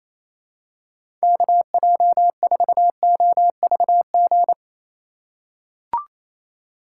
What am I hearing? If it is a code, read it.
Morse code, heard as a keyed tone, about 28 wpm, 700 Hz. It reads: KJ4OVG
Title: USA Call Signs - 2x Character Spacing - 28wpm